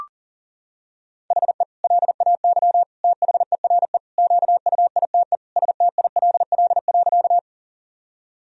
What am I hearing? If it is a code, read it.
Morse code, heard as a keyed tone, about 40 wpm, 700 Hz. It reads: HE LAY THERE QUITE STILL.
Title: Sentences from Top 1000 Words - Review Part 2 - 40wpm